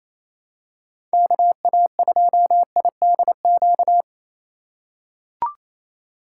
Morse code, heard as a keyed tone, about 28 wpm, 700 Hz. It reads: KA2IDQ
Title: USA Call Signs - 4x Character Spacing - 28wpm